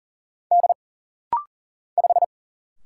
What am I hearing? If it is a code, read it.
Morse code, heard as a keyed tone, about 40 wpm, 700 Hz. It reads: D 5
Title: Single Letter-Number - Rapid-Fire 40wpm